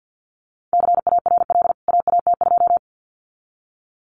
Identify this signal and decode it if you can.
Morse code, heard as a keyed tone, about 48 wpm, 700 Hz. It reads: CARL AAT2
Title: CWT Exchange 50wpm